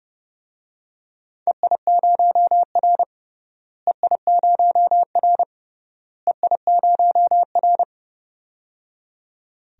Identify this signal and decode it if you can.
Morse code, heard as a keyed tone, about 30 wpm, 700 Hz. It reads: EI0R EI0R EI0R
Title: DXCC-100 Call Signs - 4 Characters - Repeated 3x - 30wpm